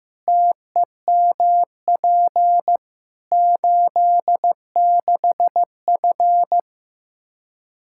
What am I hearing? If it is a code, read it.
Morse code, heard as a keyed tone, about 15 wpm, 700 Hz. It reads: TEMP 86F
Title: Practice QSO - Exchange 2 - Info - 15wpm